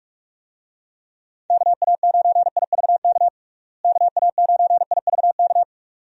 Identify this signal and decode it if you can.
Morse code, heard as a keyed tone, about 41 wpm, 700 Hz. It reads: KA9IVK KA9IVK
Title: USA Call Signs - 6 Characters - Repeated 2x - 45wpm